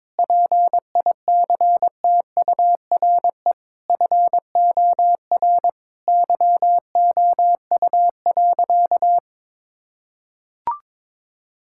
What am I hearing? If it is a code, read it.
Morse code, heard as a keyed tone, about 22 wpm, 700 Hz. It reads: PICTURE FOR YOU.
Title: Sentences from Top 1000 Words - Review Part 6 - 22wpm